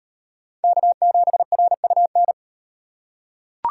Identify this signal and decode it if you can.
Morse code, heard as a keyed tone, about 38 wpm, 700 Hz. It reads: K7RUN